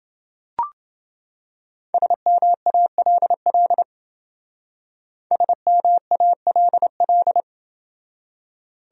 Morse code, heard as a keyed tone, about 28 wpm, 700 Hz. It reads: SMALL SMALL